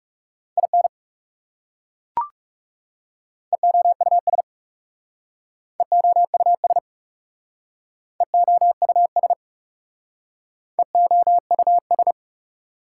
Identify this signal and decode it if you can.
Morse code, heard as a keyed tone, about 37 wpm, 700 Hz. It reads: IN EOUS EOUS EOUS EOUS
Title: Speed-Racing Prefixes and Suffixes 30wpm